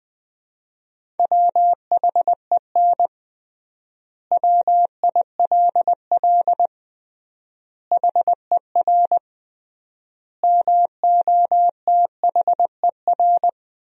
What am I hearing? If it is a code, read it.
Morse code, heard as a keyed tone, about 21 wpm, 700 Hz. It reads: WHEN WILL HER MOTHER
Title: 3x Word-Spacing - Sentences from Top 200 Words - 20wpm